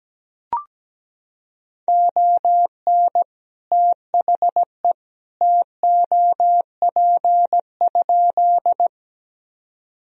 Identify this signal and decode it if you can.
Morse code, heard as a keyed tone, about 17 wpm, 700 Hz. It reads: ON THE TOP?
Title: Sentences from Top 1000 Words - Review Part 2 - 17wpm